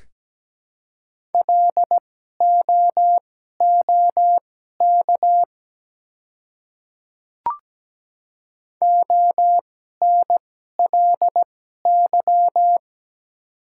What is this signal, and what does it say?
Morse code, heard as a keyed tone, about 17 wpm, spaced as for 12 wpm, 700 Hz. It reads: LOOK ONLY
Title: 2x Character Spacing - Top 200 Words - 17wpm